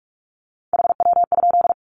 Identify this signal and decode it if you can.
Morse code, heard as a keyed tone, about 44 wpm, 700 Hz. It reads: HW?
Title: Common QSO Phrases 45wpm